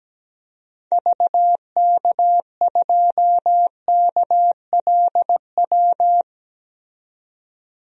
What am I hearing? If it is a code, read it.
Morse code, heard as a keyed tone, about 17 wpm, 700 Hz. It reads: VK2KLW
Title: DXCC-100 Call Signs - 6 Characters - 17wpm